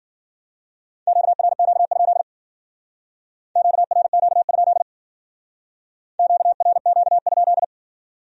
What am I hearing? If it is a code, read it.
Morse code, heard as a keyed tone, about 53 wpm, 700 Hz. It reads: QRQ? QRQ? QRQ?